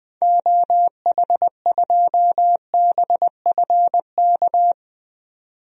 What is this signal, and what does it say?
Morse code, heard as a keyed tone, about 21 wpm, 700 Hz. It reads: OH2BFK
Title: DXCC-100 Call Signs - 6 Characters - 20wpm